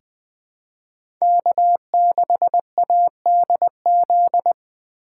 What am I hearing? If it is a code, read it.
Morse code, heard as a keyed tone, about 21 wpm, 700 Hz. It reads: K6ADZ
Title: USA Call Signs - 5 Characters - 3x Character Spacing - 20wpm